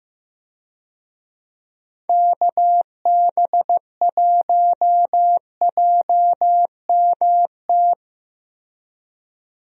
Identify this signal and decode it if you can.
Morse code, heard as a keyed tone, about 15 wpm, 700 Hz. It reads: KB1JMT
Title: USA Call Signs - 6 Characters - Repeated 3x - 15wpm